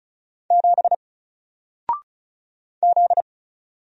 Morse code, heard as a keyed tone, about 36 wpm, 700 Hz. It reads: 7 Z
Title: Single Character (Letter:Number:Punctuation:Prosign) Alt 35wpm